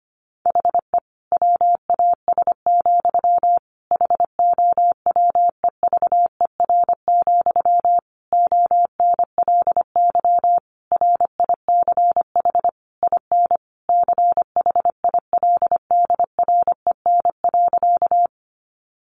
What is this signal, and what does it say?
Morse code, heard as a keyed tone, about 25 wpm, 700 Hz. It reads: HE WAS, HOWEVER, ONLY RICH IN CHILDREN.
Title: Sentences from Top 1100 Words 25wpm